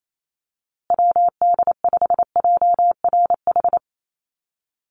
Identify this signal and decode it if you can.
Morse code, heard as a keyed tone, about 28 wpm, 700 Hz. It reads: WD5JRH
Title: USA Call Signs Repeated 2x 28wpm